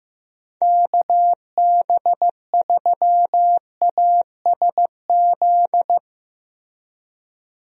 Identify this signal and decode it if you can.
Morse code, heard as a keyed tone, about 15 wpm, 700 Hz. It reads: KB3ASZ
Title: USA Call Signs - 6 Characters - Repeated 2x - 15wpm